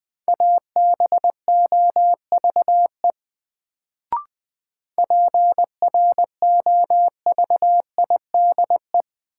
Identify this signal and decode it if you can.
Morse code, heard as a keyed tone, about 20 wpm, 700 Hz. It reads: ABOVE PROVIDE
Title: Sentences from Top 1000 Words - Review Part 2 - New Word Drill - 20wpm